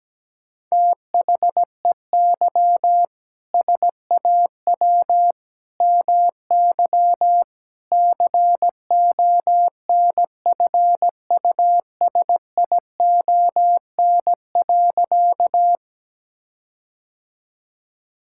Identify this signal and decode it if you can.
Morse code, heard as a keyed tone, about 17 wpm, 700 Hz. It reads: THEY SAW MY CONFUSION.
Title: Sentences from Top 1600 Words 17wpm